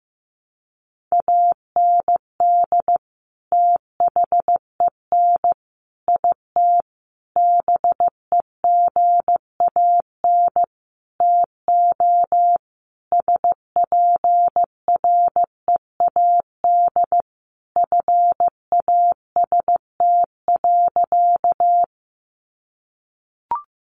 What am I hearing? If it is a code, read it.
Morse code, heard as a keyed tone, about 15 wpm, 700 Hz. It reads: AND THEN IT BEGAN TO SPREAD FAST.
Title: Sentences from Top 1200 Words 15wpm